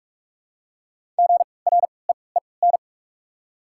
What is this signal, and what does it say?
Morse code, heard as a keyed tone, about 44 wpm, spaced as for 25 wpm, 700 Hz. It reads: GREEN
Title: 3x Character Spacing - Top 500 Words - 45wpm